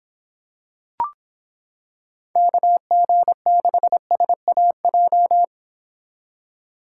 Morse code, heard as a keyed tone, about 25 wpm, 700 Hz. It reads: KG6SAJ